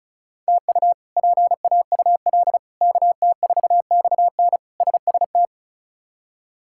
Morse code, heard as a keyed tone, about 36 wpm, 700 Hz. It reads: TU PAUL KT4XN SST
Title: K1USN SST QSO Practice - 35wpm